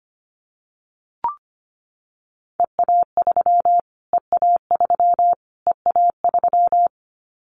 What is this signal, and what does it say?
Morse code, heard as a keyed tone, about 25 wpm, 700 Hz. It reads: EA3 EA3 EA3
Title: Callsign Prefixes for Top 25 DXCC - Repeated 3x - 25wpm